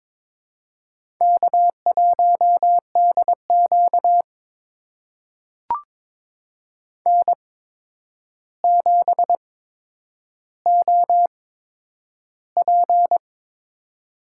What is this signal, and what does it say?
Morse code, heard as a keyed tone, about 22 wpm, spaced as for 9 wpm, 700 Hz. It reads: K1DQ N7OP
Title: USA Call Signs - 4 Characters - 8x Character Spacing - 22wpm